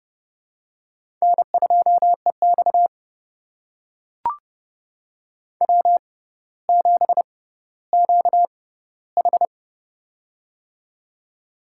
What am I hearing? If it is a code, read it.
Morse code, heard as a keyed tone, about 30 wpm, spaced as for 15 wpm, 700 Hz. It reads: N2EX W7QH
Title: USA Call Signs - 4 Characters - 6x Character Spacing - 30wpm